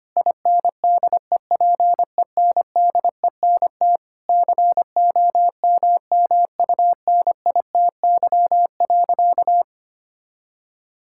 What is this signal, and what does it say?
Morse code, heard as a keyed tone, about 25 wpm, 700 Hz. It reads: INDEPENDENT COMMUNITY.
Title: Sentences from Top 1000 Words - Review Part 6 - 25wpm